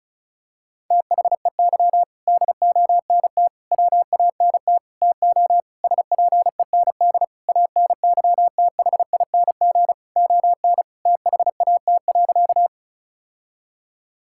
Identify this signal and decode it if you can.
Morse code, heard as a keyed tone, about 36 wpm, 700 Hz. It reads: THEY DONT WANT TO SPEND ANYTHING ON THAT.
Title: Sentences from Top 1000 Words - Review Part 5 - 35wpm